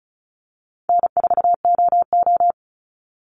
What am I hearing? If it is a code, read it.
Morse code, heard as a keyed tone, about 34 wpm, 700 Hz. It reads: N4OO
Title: Call Signs Repeated 2x - 35wpm